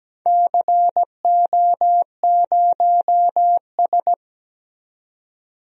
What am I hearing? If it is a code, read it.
Morse code, heard as a keyed tone, about 17 wpm, 700 Hz. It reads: CO0S
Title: DXCC-100 Call Signs - 4 Characters - 17wpm